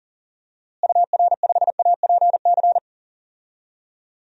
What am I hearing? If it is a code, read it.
Morse code, heard as a keyed tone, about 40 wpm, 700 Hz. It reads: UR5APC
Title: DXCC-100 Call Signs - 6 Characters - Repeated 2x - 40wpm